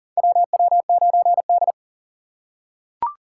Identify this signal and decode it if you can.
Morse code, heard as a keyed tone, about 39 wpm, 700 Hz. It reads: WW9D
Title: USA Call Signs - 4 Characters - 4x Character Spacing - 40wpm